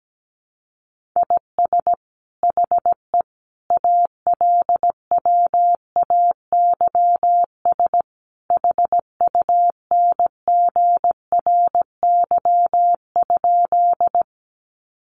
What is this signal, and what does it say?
Morse code, heard as a keyed tone, about 17 wpm, 700 Hz. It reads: IS HE ALWAYS HUNGRY?
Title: Sentences from Top 1400 Words 17wpm